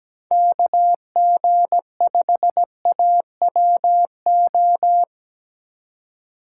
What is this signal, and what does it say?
Morse code, heard as a keyed tone, about 17 wpm, 700 Hz. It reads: KG5AWO
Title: USA Call Signs - 3x Character Spacing - 17wpm